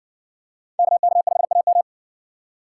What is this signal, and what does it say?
Morse code, heard as a keyed tone, about 60 wpm, 700 Hz. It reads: KK5AM